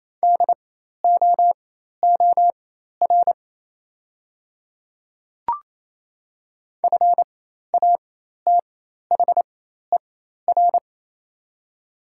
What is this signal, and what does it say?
Morse code, heard as a keyed tone, about 28 wpm, spaced as for 13 wpm, 700 Hz. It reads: DOOR FATHER
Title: 4x Character Spacing - Top 500 Words - 28wpm